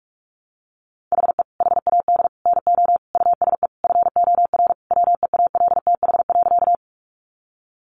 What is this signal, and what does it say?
Morse code, heard as a keyed tone, about 44 wpm, 700 Hz. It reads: HE HAD NO USE FOR WEALTH.
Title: Sentences from Top 1200 Words 45wpm